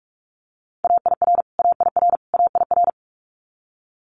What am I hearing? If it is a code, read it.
Morse code, heard as a keyed tone, about 44 wpm, 700 Hz. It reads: AIR AIR AIR
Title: Top 500 Words Repeated 45wpm